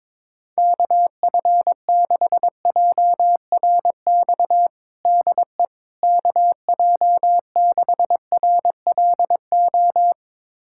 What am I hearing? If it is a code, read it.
Morse code, heard as a keyed tone, about 22 wpm, 700 Hz. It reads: KF6JRX DE KJ6RLO
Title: USA Callsign DE Callsign 22wpm